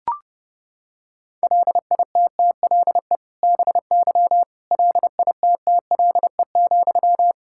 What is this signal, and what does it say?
Morse code, heard as a keyed tone, about 30 wpm, 700 Hz. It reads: LITTLE BY LITTLE,
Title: Sentences from Top 1600 Words 30wpm